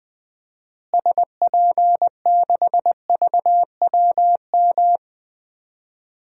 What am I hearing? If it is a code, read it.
Morse code, heard as a keyed tone, about 20 wpm, 700 Hz. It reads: SP6VWM
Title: DXCC-100 Call Signs - 6 Characters - Repeated 2x - 20wpm